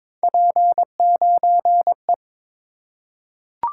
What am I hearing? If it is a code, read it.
Morse code, heard as a keyed tone, about 22 wpm, 700 Hz. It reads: P9E
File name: practice / 22wpm / Three Letter-Number Combinations - 22wpm